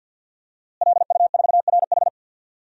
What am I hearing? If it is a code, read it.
Morse code, heard as a keyed tone, about 52 wpm, 700 Hz. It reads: LU4RH
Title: DXCC-100 Call Signs - 5 Characters - Repeated 2x - 50wpm